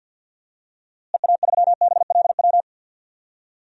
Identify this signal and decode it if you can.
Morse code, heard as a keyed tone, about 50 wpm, 700 Hz. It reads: EA3BLW
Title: DXCC-100 Call Signs - 6 Characters - Repeated 3x - 50wpm